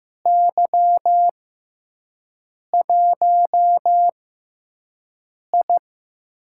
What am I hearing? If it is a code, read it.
Morse code, heard as a keyed tone, about 15 wpm, spaced as for 5 wpm, 700 Hz. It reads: Y1I
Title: USA Call Signs - 4 Characters - 6x Character Spacing - 15wpm